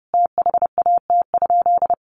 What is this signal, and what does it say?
Morse code, heard as a keyed tone, about 30 wpm, 700 Hz. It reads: THAT?
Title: Sentences from Top 1400 Words 30wpm